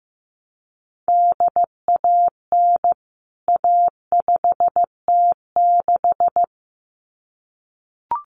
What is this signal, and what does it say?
Morse code, heard as a keyed tone, about 15 wpm, 700 Hz. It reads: DAN A5T6
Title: CWT Exchange 15wpm